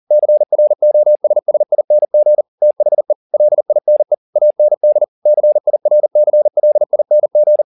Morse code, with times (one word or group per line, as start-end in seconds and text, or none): 0.1-2.4 CROSSING
2.6-3.1 THE
3.3-4.1 LINE
4.4-5.0 AND
5.3-7.6 CIRCLING